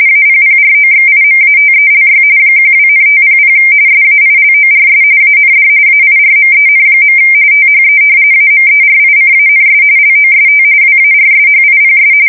RTTY, about 45.45 baud. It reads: APPROXIMATELY 12,500 SAILORS AND AIRMEN. THE AMERICANS LOST WARSHIPS
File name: fsk170 / battleshipnj